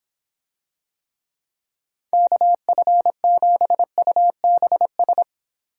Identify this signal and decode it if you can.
Morse code, heard as a keyed tone, about 25 wpm, 700 Hz. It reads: KF7UBS